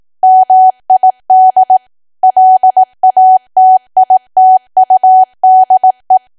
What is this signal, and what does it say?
Morse code, heard as a keyed tone, about 18 wpm, 750 Hz. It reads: MID LATITUDE
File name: cwa / W1AW-Level2 / 18WPM